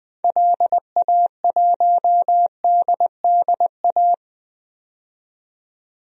Morse code, heard as a keyed tone, about 20 wpm, 700 Hz. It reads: LA1DDA
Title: DXCC-100 Call Signs - 6 Characters - 20wpm